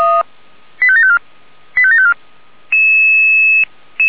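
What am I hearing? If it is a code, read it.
Five-tone selective call (ZVEI), heard as quick runs of five tones, tones of about 70 ms. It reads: 85754 85754
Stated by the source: ZVEI five-tone call received on SDR.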